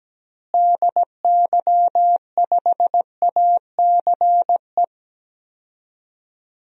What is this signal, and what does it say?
Morse code, heard as a keyed tone, about 17 wpm, 700 Hz. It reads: DY5ACE
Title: DXCC-100 Call Signs - 6 Characters - Repeated 2x - 17wpm